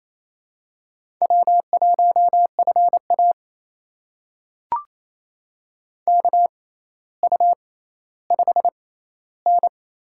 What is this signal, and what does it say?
Morse code, heard as a keyed tone, about 28 wpm, spaced as for 14 wpm, 700 Hz. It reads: W1FA KU5N
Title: USA Call Signs - 4 Characters - 6x Character Spacing - 28wpm